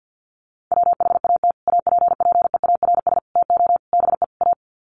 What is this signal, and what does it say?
Morse code, heard as a keyed tone, about 48 wpm, 700 Hz. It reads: WHAT APPEARS TO BE A
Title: Sets of 5 Words Encore2 50wpm